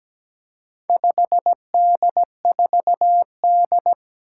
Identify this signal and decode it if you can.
Morse code, heard as a keyed tone, about 17 wpm, 700 Hz. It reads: 5D4D
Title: Four Challenging Letter-Number Combinations 17wpm